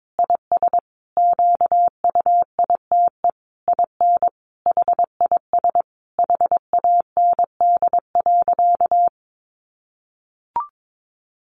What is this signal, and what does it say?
Morse code, heard as a keyed tone, about 22 wpm, 700 Hz. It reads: IS QUITE IN HIS HAND.
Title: Sentences from Top 1400 Words 22wpm